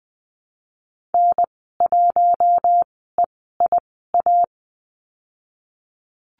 Morse code, heard as a keyed tone, about 20 wpm, spaced as for 14 wpm, 700 Hz. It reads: N1EIA
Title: 062 Lesson 11g - Letter h - Call Signs 20wpm